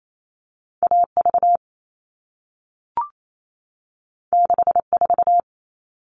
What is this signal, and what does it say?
Morse code, heard as a keyed tone, about 28 wpm, 700 Hz. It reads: AV 64
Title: All Cut Numbers - 2-Digit 28wpm